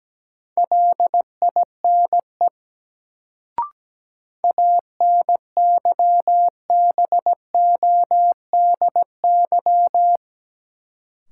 Morse code, heard as a keyed tone, about 17 wpm, 700 Hz. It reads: LINE ANYBODY
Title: Sets of 5 Words - New Word Drill - 17wpm